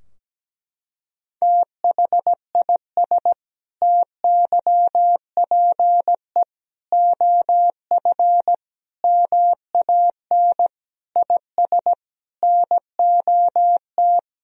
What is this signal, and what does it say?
Morse code, heard as a keyed tone, about 17 wpm, 700 Hz. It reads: THIS TYPE OF MAN IS NOT
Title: Sentences from Top 1000 Words - Review Part 3 - 17wpm